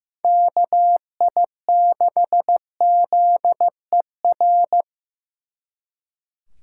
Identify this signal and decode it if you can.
Morse code, heard as a keyed tone, about 18 wpm, 700 Hz. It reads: KI6ZER
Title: Speed Racing - USA 2x3 Call Signs 15wpm